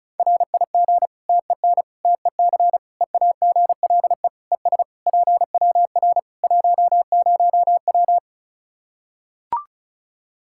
Morse code, heard as a keyed tone, about 36 wpm, 700 Hz. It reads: RIG TEN TEC EAGLE ES PWR 10W
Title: K1USN SST QSO Practice 35wpm